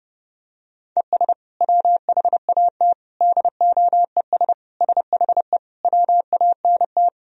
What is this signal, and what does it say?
Morse code, heard as a keyed tone, about 30 wpm, 700 Hz. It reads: ES WHAT DOES SHE WANT
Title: Sentences from Top 400 Words and Abbreviations 30wpm